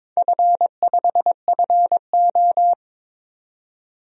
Morse code, heard as a keyed tone, about 22 wpm, 700 Hz. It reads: F5FO
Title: DXCC-100 Call Signs - 4 Characters - 22wpm